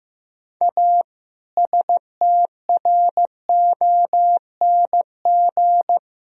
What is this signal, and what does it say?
Morse code, heard as a keyed tone, about 16 wpm, 700 Hz. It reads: A STRONG
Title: Sentences from Top 1000 Words - Review Part 3 - 15wpm